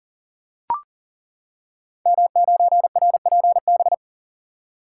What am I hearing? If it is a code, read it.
Morse code, heard as a keyed tone, about 42 wpm, 700 Hz. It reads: M9RPB
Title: Code Groups - 5 Letter:Numbers - 40wpm